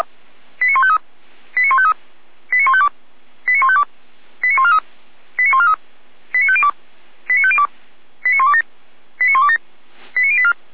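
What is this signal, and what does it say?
Five-tone selective call (ZVEI), heard as quick runs of five tones, tones of about 70 ms. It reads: 89153 89153 89152 89152 89143 89143 89502 89502 89127 89127 89084
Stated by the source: Five-tone ZVEI selective call received on an SDR.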